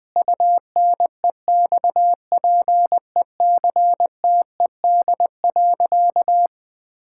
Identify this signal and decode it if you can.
Morse code, heard as a keyed tone, about 20 wpm, 700 Hz. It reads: UNEXPECTED.
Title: Sentences with 2-Word Phrases - 20wpm